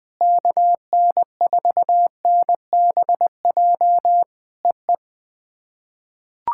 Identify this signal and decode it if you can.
Morse code, heard as a keyed tone, about 20 wpm, 700 Hz. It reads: KN4NBJ EE
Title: POTA QSO Practice - 20wpm